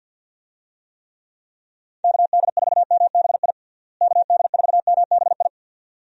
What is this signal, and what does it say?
Morse code, heard as a keyed tone, about 51 wpm, 700 Hz. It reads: KD4MBI KD4MBI
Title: USA Call Signs - 6 Characters - Repeated 2x - 50wpm